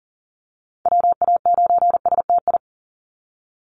Morse code, heard as a keyed tone, about 40 wpm, 700 Hz. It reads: WA9STI
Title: NA SOTA Activator Callsigns 40wpm